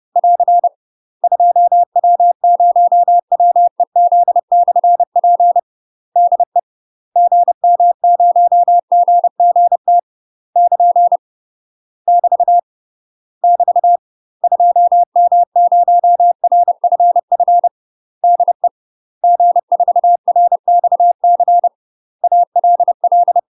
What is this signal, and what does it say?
Morse code, heard as a keyed tone, about 30 wpm, 700 Hz. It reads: <AR> 2W0WEZ/P DE GM0GGT <KN> <BT> <BT> 2M0RFF DE G4RXC ALL
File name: UK MASTER QSOs TRAINING / 30WPM / UK MASTER QSOs 0050